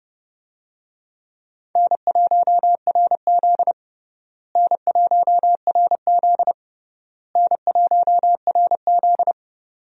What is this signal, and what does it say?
Morse code, heard as a keyed tone, about 30 wpm, 700 Hz. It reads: N1RZ N1RZ N1RZ
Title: USA Call Signs - 4 Characters - Repeated 3x - 30wpm